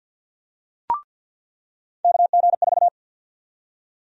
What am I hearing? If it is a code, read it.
Morse code, heard as a keyed tone, about 47 wpm, 700 Hz. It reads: KG4
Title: USA Call Sign Prefixes - 50wpm